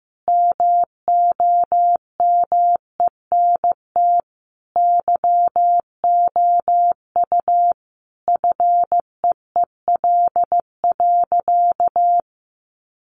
Morse code, heard as a keyed tone, about 15 wpm, 700 Hz. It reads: MOMENT YOU FEEL.
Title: Sentences from Top 1000 Words 15wpm